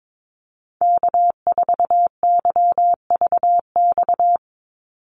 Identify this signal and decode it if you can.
Morse code, heard as a keyed tone, about 22 wpm, 700 Hz. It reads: K4YVX
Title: USA Call Signs 22wpm